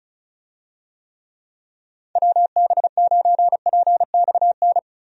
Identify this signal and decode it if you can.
Morse code, heard as a keyed tone, about 36 wpm, 700 Hz. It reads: WB9PXN
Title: USA Call Signs - 6 Characters - Repeated 3x - 35wpm